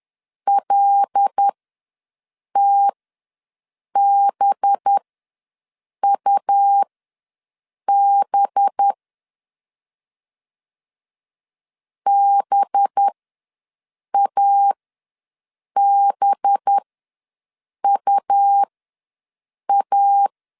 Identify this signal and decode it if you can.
Morse code, heard as a keyed tone, about 10 wpm, spaced as for 6 wpm, 800 Hz. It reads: LTBUB BABUA
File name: LEZIONI / 5 LBVC / 5 LB ws 10wpm 3616